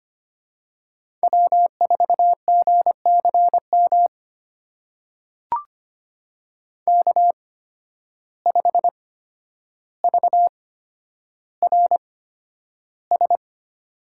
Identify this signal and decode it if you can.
Morse code, heard as a keyed tone, about 25 wpm, spaced as for 11 wpm, 700 Hz. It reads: W4GCM K5VRS
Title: USA Call Signs - 5 Characters - 8x Character Spacing - 25wpm